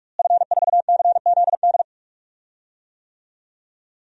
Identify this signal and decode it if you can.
Morse code, heard as a keyed tone, about 46 wpm, 700 Hz. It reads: F4CZD
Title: DXCC-100 Call Signs - 5 Characters - Repeated 2x - 45wpm